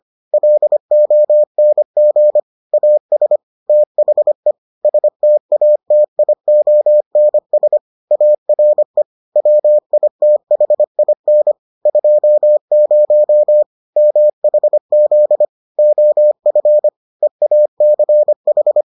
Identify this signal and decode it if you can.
Morse code, heard as a keyed tone, about 25 wpm, 600 Hz. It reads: LONG AS THE STATIONS ARE WITHIN 20 MHZ OF EACH